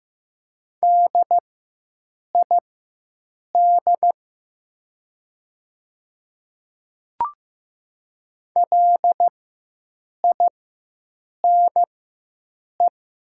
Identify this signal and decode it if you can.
Morse code, heard as a keyed tone, about 15 wpm, spaced as for 7 wpm, 700 Hz. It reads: DID LINE
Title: 4x Character Spacing - Top 300 Words - 15wpm